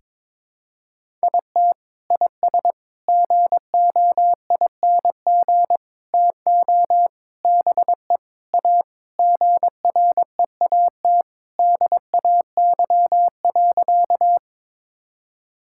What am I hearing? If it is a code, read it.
Morse code, heard as a keyed tone, about 22 wpm, 700 Hz. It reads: IT IS GOING TO BE A GREAT DAY.
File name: practice / 22wpm / Sentences with 3-Word Phrases - 22wpm